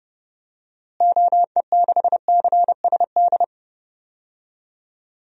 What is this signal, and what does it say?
Morse code, heard as a keyed tone, about 32 wpm, 700 Hz. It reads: OE6CSD
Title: DXCC-100 Call Signs - 6 Characters - Repeated 2x - 30wpm